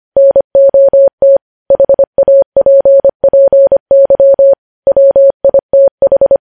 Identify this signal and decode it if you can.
Morse code, heard as a keyed tone, about 25 wpm, 550 Hz. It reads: NOT HAPPY WITH
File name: podcast / News headlines-1x-25x25